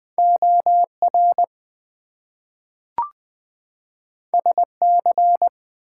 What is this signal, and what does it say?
Morse code, heard as a keyed tone, about 20 wpm, 700 Hz. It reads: OR SC
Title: US States and Canadian Provinces 20wpm